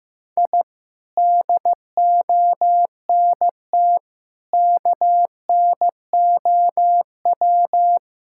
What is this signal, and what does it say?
Morse code, heard as a keyed tone, about 15 wpm, 700 Hz. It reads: I DONT KNOW
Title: Sentences from Top 1000 Words - Review Part 5 - 15wpm